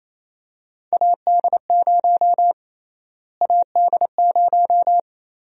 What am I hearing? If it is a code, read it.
Morse code, heard as a keyed tone, about 28 wpm, 700 Hz. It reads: AD0 AD0
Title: USA Call Sign Prefixes - Repeated 2x - 28wpm